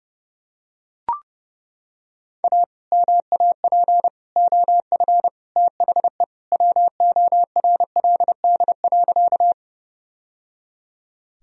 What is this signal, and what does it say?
Morse code, heard as a keyed tone, about 30 wpm, 700 Hz. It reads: A MAP OF THE WORLD.
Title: Sentences from Top 1000 Words - Review Part 2 - 30wpm